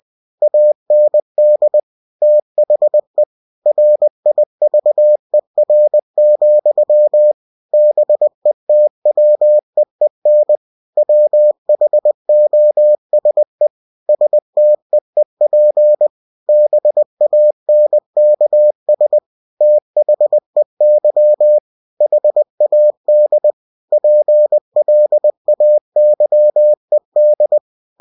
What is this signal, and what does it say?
Morse code, heard as a keyed tone, about 20 wpm, 600 Hz. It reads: AND THE RIVER, BETWEEN WHOSE STEEP BANKS THEY HAD PLAYED
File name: cw-books / winnie-the-pooh / winnie-20wpm-page-0091